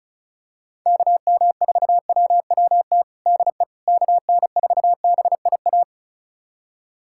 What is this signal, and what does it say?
Morse code, heard as a keyed tone, about 34 wpm, 700 Hz. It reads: KM4WWT DE KN4BIA
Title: USA Callsign DE Callsign 35wpm